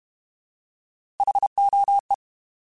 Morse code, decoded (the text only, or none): HOE